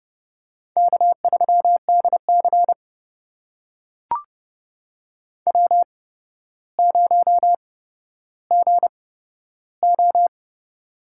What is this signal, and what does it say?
Morse code, heard as a keyed tone, about 31 wpm, spaced as for 13 wpm, 700 Hz. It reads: K3DC W0GO
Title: USA Call Signs - 4 Characters - 8x Character Spacing - 30wpm